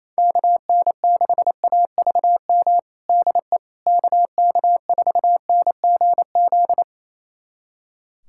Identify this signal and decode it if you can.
Morse code, heard as a keyed tone, about 28 wpm, 700 Hz. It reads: KN6AVM DE KK4NGZ
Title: USA Callsign DE Callsign 28wpm